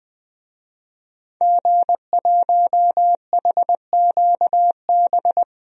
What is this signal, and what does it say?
Morse code, heard as a keyed tone, about 20 wpm, 700 Hz. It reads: G1HQB
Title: DXCC-100 Call Signs - 5 Characters - Repeated 3x - 20wpm